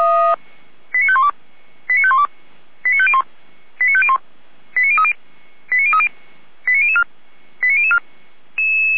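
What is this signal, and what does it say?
Five-tone selective call (ZVEI), heard as quick runs of five tones, tones of about 70 ms. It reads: 89512 89512 89501 89501 89030 89030 89004 89004
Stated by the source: ZVEI five-tone call received on SDR, from feuerwehr-orlamuende.org fire brigade server.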